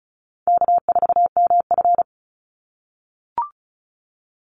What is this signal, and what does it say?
Morse code, heard as a keyed tone, about 34 wpm, 700 Hz. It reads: K4MF
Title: NA SOTA Chaser Callsigns 35wpm